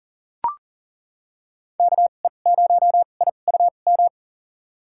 Morse code, heard as a keyed tone, about 40 wpm, spaced as for 29 wpm, 700 Hz. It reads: KE0IUM
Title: USA Call Signs - 6 Characters - 2x Character Spacing - 40wpm